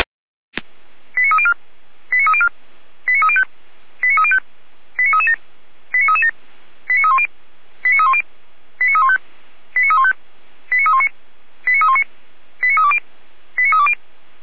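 Five-tone selective call (ZVEI), heard as quick runs of five tones, tones of about 70 ms. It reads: 89304 89304 89305 89305 89337 89337 89310 89310 89315 89315 89319 89319 89320 89320
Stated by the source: ZVEI five-tone selective call received by SDR, recorded on the audio server of Feuerwehr Orlamünde.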